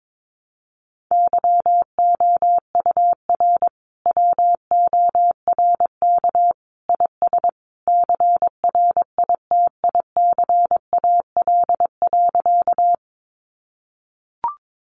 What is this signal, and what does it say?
Morse code, heard as a keyed tone, about 22 wpm, 700 Hz. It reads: YOUR WORK IS CRITICAL.
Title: Sentences from Top 1100 Words 22wpm